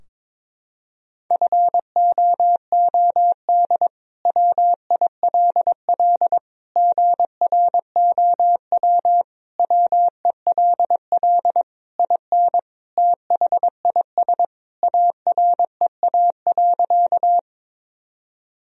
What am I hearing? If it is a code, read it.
Morse code, heard as a keyed tone, about 22 wpm, 700 Hz. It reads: FOOD WILL GROW WELL IN THIS AREA.
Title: Sentences from Top 1000 Words - Review Part 2 - 22wpm